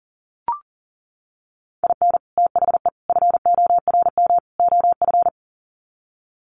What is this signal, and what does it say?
Morse code, heard as a keyed tone, about 40 wpm, 700 Hz. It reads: IN THE FORM OF
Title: Sets of 4 Words 40wpm